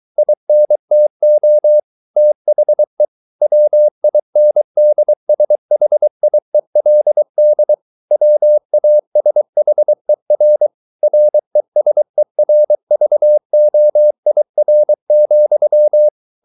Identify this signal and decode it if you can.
Morse code, heard as a keyed tone, about 23 wpm, 600 Hz. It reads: INTO THE WINDSHIELD WASHER RESERVOIR,
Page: Romanen Never af Ken Follet på engelsk.